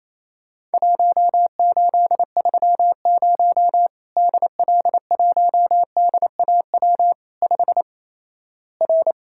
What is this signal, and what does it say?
Morse code, heard as a keyed tone, about 28 wpm, 700 Hz. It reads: 1830 DL1DAW 5 R
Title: Worked All Europe DX Contest - QTC Exchange - 28wpm